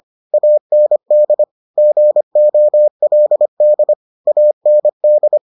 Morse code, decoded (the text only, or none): AND GOLD AND